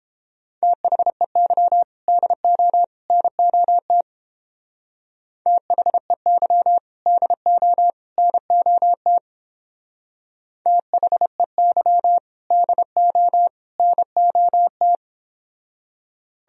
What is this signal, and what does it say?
Morse code, heard as a keyed tone, about 30 wpm, 700 Hz. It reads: THEY DO NOT THEY DO NOT THEY DO NOT